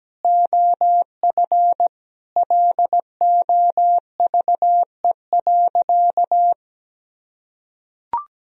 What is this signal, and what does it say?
Morse code, heard as a keyed tone, about 17 wpm, 700 Hz. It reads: OF LOVE.
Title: Sentences from Top 1000 Words - Review Part 2 - 17wpm